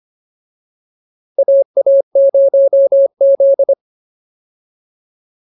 Morse code, heard as a keyed tone, about 25 wpm, 550 Hz. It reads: AA0Z